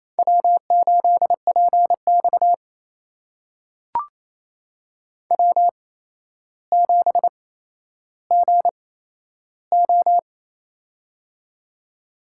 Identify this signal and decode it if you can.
Morse code, heard as a keyed tone, about 28 wpm, spaced as for 12 wpm, 700 Hz. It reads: W8PX W7GO
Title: USA Call Signs - 4 Characters - 8x Character Spacing - 28wpm